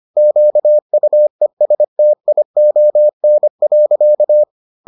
Morse code, heard as a keyed tone, about 25 wpm, 600 Hz. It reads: QUESTION.